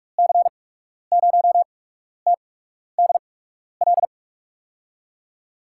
Morse code, heard as a keyed tone, about 46 wpm, spaced as for 12 wpm, 700 Hz. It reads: C0TDL
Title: USA Call Signs - 8x Character Spacing - 45wpm